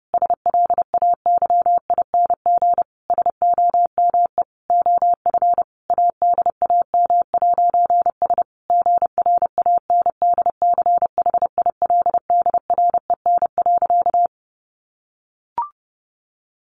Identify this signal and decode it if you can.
Morse code, heard as a keyed tone, about 28 wpm, 700 Hz. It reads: SLAYING SOME OF ADAM'S GRANDCHILDREN.
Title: A Sound Of Thunder 30wpm